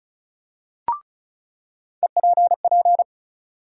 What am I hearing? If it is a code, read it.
Morse code, heard as a keyed tone, about 36 wpm, 700 Hz. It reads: EPP
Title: USA Call Sign Postfixes - 3 Letters 35wpm